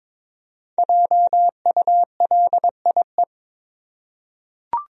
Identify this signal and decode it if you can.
Morse code, heard as a keyed tone, about 22 wpm, 700 Hz. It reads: JULIE
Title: USA Common Names - Max 5 Letters - 22wpm